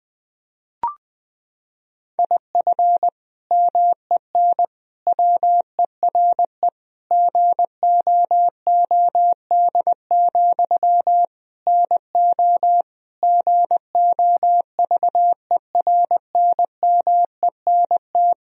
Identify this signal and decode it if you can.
Morse code, heard as a keyed tone, about 21 wpm, 700 Hz. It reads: IF MEN WERE GOOD, NO GOVERNMENT
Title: Sentences from Top 1000 Words - Review Part 3 - 20wpm